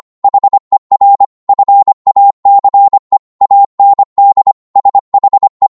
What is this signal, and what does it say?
Morse code, heard as a keyed tone, about 25 wpm, 800 Hz. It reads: HER FACE AND SHE